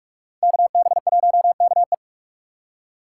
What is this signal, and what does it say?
Morse code, heard as a keyed tone, about 46 wpm, 700 Hz. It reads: KB1KE
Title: USA Call Signs - 5 Characters - 2x Character Spacing - 45wpm